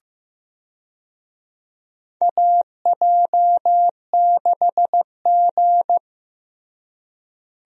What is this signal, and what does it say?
Morse code, heard as a keyed tone, about 15 wpm, 700 Hz. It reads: AJ6G
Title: USA Call Signs - 4 Characters - Repeated 2x - 15wpm